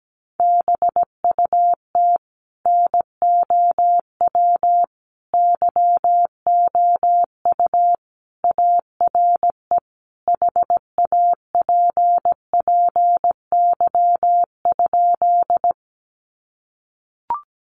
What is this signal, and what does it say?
Morse code, heard as a keyed tone, about 17 wpm, 700 Hz. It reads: BUT NOW YOU ARE HAPPY?
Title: Sentences from Top 1000 Words 17wpm